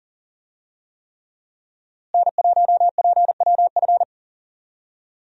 Morse code, heard as a keyed tone, about 40 wpm, 700 Hz. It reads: N1PWF
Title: USA Call Signs - 5 Characters - Repeated 3x - 40wpm